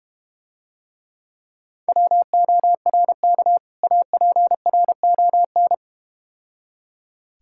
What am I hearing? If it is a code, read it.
Morse code, heard as a keyed tone, about 33 wpm, 700 Hz. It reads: WORK APRON